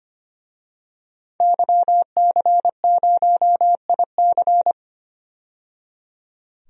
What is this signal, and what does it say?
Morse code, heard as a keyed tone, about 25 wpm, 700 Hz. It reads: YC0IC
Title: DXCC-100 Call Signs - 5 Characters - 25wpm